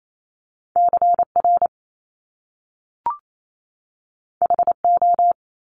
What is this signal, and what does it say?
Morse code, heard as a keyed tone, about 28 wpm, 700 Hz. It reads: CR HO
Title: Top Two Letter Combinations in Words 28wpm